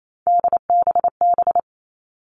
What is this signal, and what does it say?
Morse code, heard as a keyed tone, about 28 wpm, 700 Hz. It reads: DBB
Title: All Cut Numbers - 3-Digit 28wpm